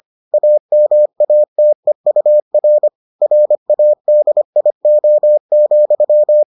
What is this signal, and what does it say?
Morse code, heard as a keyed tone, about 25 wpm, 600 Hz. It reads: AMATEUR RADIO,